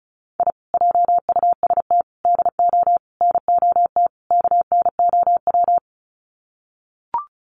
Morse code, heard as a keyed tone, about 34 wpm, 700 Hz. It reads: I JUST DO NOT KNOW
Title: Sets of 5 Words Encore 35wpm